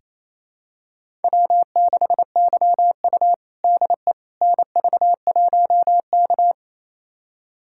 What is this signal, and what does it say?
Morse code, heard as a keyed tone, about 28 wpm, 700 Hz. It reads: W6YU DE NV1K
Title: USA Callsign DE Callsign - 4 Characters - 28wpm